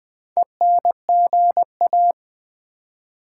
Morse code, heard as a keyed tone, about 20 wpm, 700 Hz. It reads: ENGA
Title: Build A Word - 2 Letters At A Time - 8-Letter Words - 20wpm